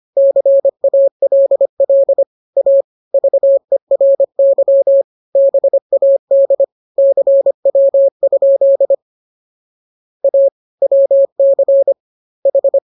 Morse code, heard as a keyed tone, about 25 wpm, 550 Hz. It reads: CALL A VERY BAD CW? A WC H
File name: jokes / joke 8 25 0000